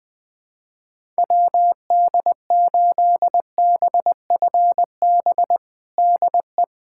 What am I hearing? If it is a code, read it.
Morse code, heard as a keyed tone, about 20 wpm, 700 Hz. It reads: WD8BFB DE
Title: USA Callsign DE Callsign - 6 Characters - 20wpm